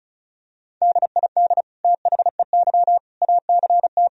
Morse code, heard as a keyed tone, about 34 wpm, 700 Hz. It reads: DID THEY ACT
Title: Sentences from Top 1000 Words - Review Part 4 - 35wpm